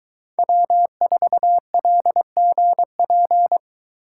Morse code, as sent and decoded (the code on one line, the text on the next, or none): .-- ....- .-.. --. .--.
W4LGP